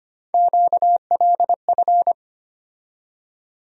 Morse code, heard as a keyed tone, about 25 wpm, 700 Hz. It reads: QLF
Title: USA Call Sign Postfixes - 3 Letters 25wpm